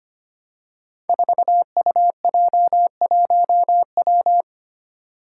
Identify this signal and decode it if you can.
Morse code, heard as a keyed tone, about 25 wpm, 700 Hz. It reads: 4UJ1W
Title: Five Challenging Letter-Number Combinations 25wpm